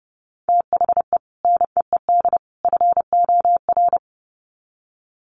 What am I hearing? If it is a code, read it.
Morse code, heard as a keyed tone, about 30 wpm, 700 Hz. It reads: THE NEED FOR
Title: Sets of 3 Words 30wpm